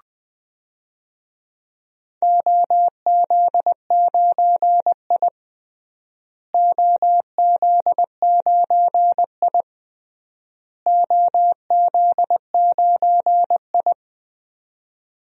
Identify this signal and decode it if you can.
Morse code, heard as a keyed tone, about 20 wpm, 700 Hz. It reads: OZ9I OZ9I OZ9I
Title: DXCC-100 Call Signs - 4 Characters - Repeated 3x - 20wpm